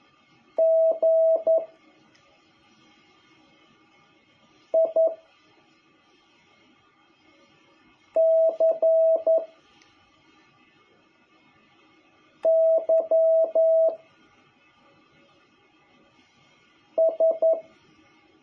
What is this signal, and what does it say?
Morse code, heard as a keyed tone, about 11 wpm, 650 Hz. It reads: G I C Y S
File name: jh8cbh / 2021cw